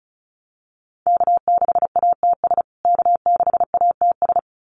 Morse code, heard as a keyed tone, about 34 wpm, 700 Hz. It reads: K6ATS K6ATS
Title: USA Call Signs Repeated 2x 35wpm